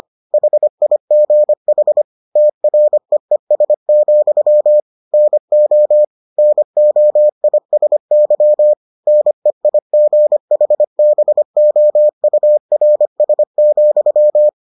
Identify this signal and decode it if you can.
Morse code, heard as a keyed tone, about 25 wpm, 600 Hz. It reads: HIGH TREES, NO NOISY NEIGHBOURS,